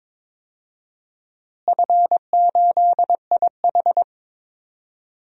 Morse code, heard as a keyed tone, about 22 wpm, 700 Hz. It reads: F8IH
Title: DXCC-100 Call Signs - 4 Characters - Repeated 3x - 22wpm